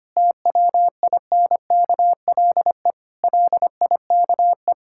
Morse code, heard as a keyed tone, about 25 wpm, 700 Hz. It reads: TWINKLE LIKE
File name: practice / 25wpm / Difficult Sentences with Hints - 25wpm